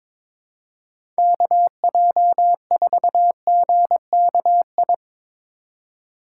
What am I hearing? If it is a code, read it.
Morse code, heard as a keyed tone, about 22 wpm, 700 Hz. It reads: KJ4GKI
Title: USA Call Signs - 6 Characters - 2x Character Spacing - 22wpm